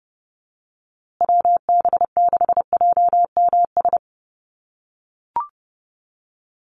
Morse code, heard as a keyed tone, about 29 wpm, 700 Hz. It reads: WB6JMS
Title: USA Call Signs Repeated 2x 30wpm